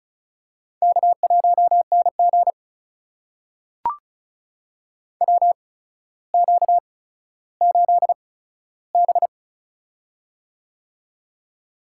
Morse code, heard as a keyed tone, about 36 wpm, spaced as for 15 wpm, 700 Hz. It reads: K1NG WQ8B
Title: USA Call Signs - 4 Characters - 8x Character Spacing - 35wpm